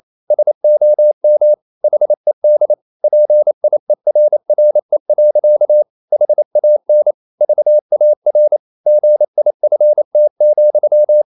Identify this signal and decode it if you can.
Morse code, heard as a keyed tone, about 28 wpm, 600 Hz. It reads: SOM HED PIERRE. HAN VAR GIFT,
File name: Ferie QRQ 28wpm